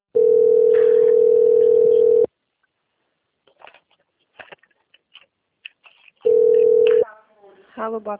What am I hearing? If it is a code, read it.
Ringback tone (North America), 440 + 480 Hz, with speech.
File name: out- - - - - .16363